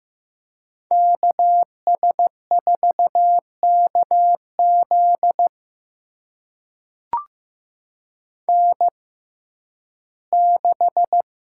Morse code, heard as a keyed tone, about 15 wpm, spaced as for 11 wpm, 700 Hz. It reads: KS4KZ N6
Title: USA Call Signs - 6x Character Spacing - 15wpm